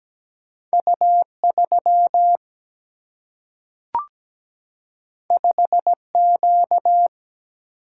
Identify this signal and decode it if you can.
Morse code, heard as a keyed tone, about 17 wpm, 700 Hz. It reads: U3 5Q
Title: Single-Letter Single-Number 17wpm